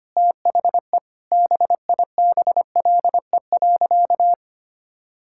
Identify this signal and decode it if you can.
Morse code, heard as a keyed tone, about 25 wpm, 700 Hz. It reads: THE BIBLE.
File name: practice / 25wpm / Sentences with 9-Word Phrases - 25wpm